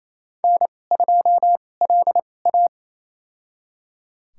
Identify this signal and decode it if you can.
Morse code, heard as a keyed tone, about 28 wpm, spaced as for 20 wpm, 700 Hz. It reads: N2LA
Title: USA Call Signs - 4 Characters - 2x Character Spacing - 28wpm